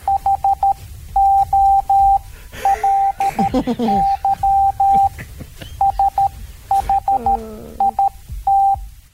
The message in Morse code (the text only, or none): HOLY SHIT